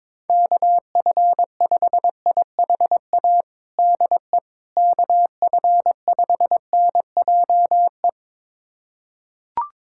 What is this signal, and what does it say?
Morse code, heard as a keyed tone, about 22 wpm, 700 Hz. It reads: KF5IHA DE KF5NJE
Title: USA Callsign DE Callsign 22wpm